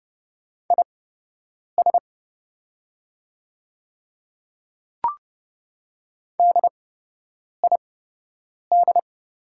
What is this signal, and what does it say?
Morse code, heard as a keyed tone, about 32 wpm, spaced as for 8 wpm, 700 Hz. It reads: IS DID
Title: 8x Character Spacing - Top 100 Words - 30wpm